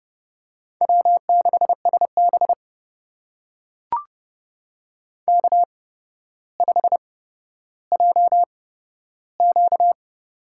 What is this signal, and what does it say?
Morse code, heard as a keyed tone, about 29 wpm, spaced as for 13 wpm, 700 Hz. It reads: W6SB K5JQ
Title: USA Call Signs - 4 Characters - 8x Character Spacing - 30wpm